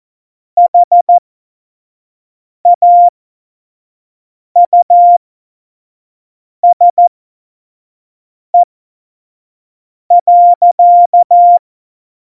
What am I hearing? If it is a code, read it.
Morse code, heard as a keyed tone, about 14 wpm, spaced as for 5 wpm, 700 Hz. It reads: HAUSE.
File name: Morsekurs / Witze / Witz 27 Juni 17, Tempo 25 F70